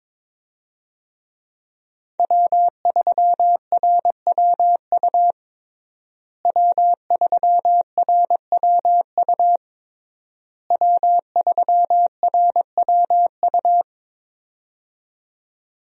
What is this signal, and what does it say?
Morse code, heard as a keyed tone, about 22 wpm, 700 Hz. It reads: W3RWU W3RWU W3RWU
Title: USA Call Signs - 5 Characters - Repeated 3x - 22wpm